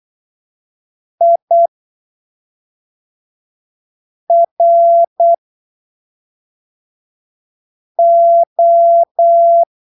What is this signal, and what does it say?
Morse code, heard as a keyed tone, about 8 wpm, spaced as for 3 wpm, 680 Hz. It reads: IRO
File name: arquivos / texto / texto 16 08ppm